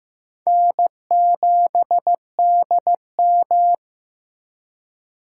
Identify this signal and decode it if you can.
Morse code, heard as a keyed tone, about 15 wpm, 700 Hz. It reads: N7DM
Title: USA Call Signs - 4 Characters - 2x Character Spacing - 15wpm